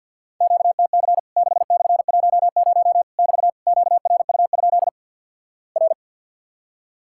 Morse code, heard as a keyed tone, about 50 wpm, 700 Hz. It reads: QTC 6/10 <BT> QRU? R
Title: Worked All Europe DX Contest - QTC Exchange - 50wpm